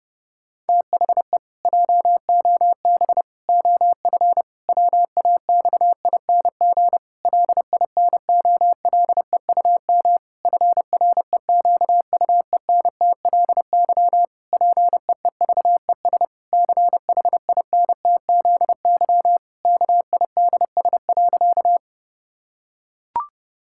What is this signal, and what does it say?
Morse code, heard as a keyed tone, about 30 wpm, 700 Hz. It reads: THE JOB OF WAXING LINOLEUM FREQUENTLY PEEVES CHINTZY KIDS.
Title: Pangram Challenge v2 - 30wpm